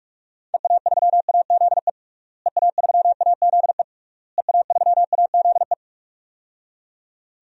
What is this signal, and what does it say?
Morse code, heard as a keyed tone, about 44 wpm, 700 Hz. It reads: EA3AZE EA3AZE EA3AZE
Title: DXCC-100 Call Signs - 6 Characters - Repeated 3x - 45wpm